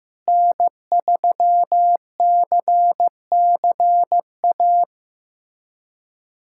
Morse code, heard as a keyed tone, about 15 wpm, 700 Hz. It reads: N3CCA
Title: USA Callsign DE Callsign - 5 Characters - 15wpm